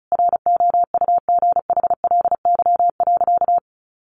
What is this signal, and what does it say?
Morse code, heard as a keyed tone, about 34 wpm, 700 Hz. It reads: ROUGHLY.
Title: Sentences from Top 1500 Words 35wpm